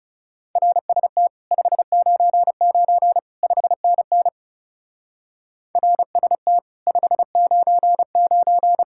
Morse code, heard as a keyed tone, about 40 wpm, 700 Hz. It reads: RST 599 5NN RST 599